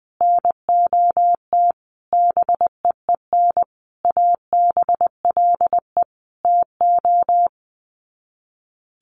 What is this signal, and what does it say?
Morse code, heard as a keyed tone, about 20 wpm, 700 Hz. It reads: NOT BEEN ABLE TO
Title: Sets of 5 Words Encore 20wpm